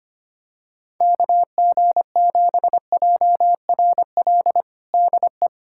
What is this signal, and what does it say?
Morse code, heard as a keyed tone, about 25 wpm, 700 Hz. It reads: KG7JRL DE
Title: USA Callsign DE Callsign - 6 Characters - 25wpm